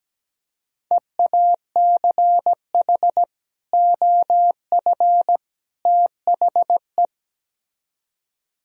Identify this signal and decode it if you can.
Morse code, heard as a keyed tone, about 17 wpm, 700 Hz. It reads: EACH OF THE
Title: Speed-Racing - Sets of 3 Words 17wpm